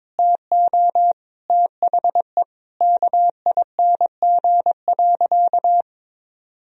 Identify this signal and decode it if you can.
Morse code, heard as a keyed tone, about 22 wpm, 700 Hz. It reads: TO THE KING.
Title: Sentences from Top 1000 Words - Review Part 4 - 22wpm